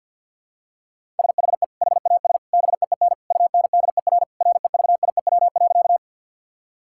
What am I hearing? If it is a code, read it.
Morse code, heard as a keyed tone, about 52 wpm, 700 Hz. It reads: SHE HAS BEEN UNDER REVIEW.
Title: Sentences from Top 1000 Words - Review Part 6 - 50wpm